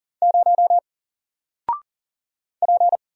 Morse code, heard as a keyed tone, about 40 wpm, 700 Hz. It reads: 0 P
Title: Single Character (Letter:Number:Punctuation:Prosign) Alt 40wpm